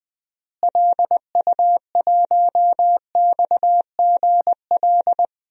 Morse code, heard as a keyed tone, about 20 wpm, 700 Hz. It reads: LU1XGL
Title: DXCC-100 Call Signs - 6 Characters - Repeated 3x - 20wpm